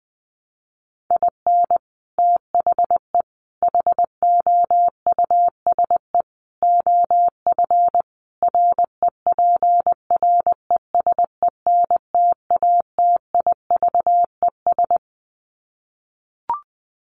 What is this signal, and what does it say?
Morse code, heard as a keyed tone, about 20 wpm, 700 Hz. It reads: IN THE HOUSE OF REPRESENTATIVES
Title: Sets of 5 Words Encore 20wpm